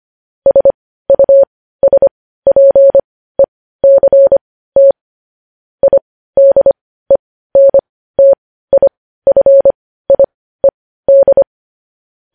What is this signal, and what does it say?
Morse code, heard as a keyed tone, about 25 wpm, spaced as for 15 wpm, 550 Hz. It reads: SUSPECT IDENTIFIED